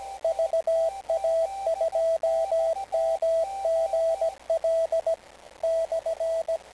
Morse code, heard as a keyed tone, about 17 wpm, 660 Hz, with radio noise. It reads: VA2MGL /
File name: kl7fdq / beacons / va2mgl-lu4jj